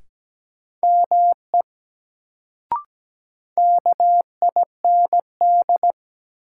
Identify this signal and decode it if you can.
Morse code, heard as a keyed tone, about 17 wpm, 700 Hz. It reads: ME KIND
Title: Sentences from Top 200 Words - New Word Drill - 17wpm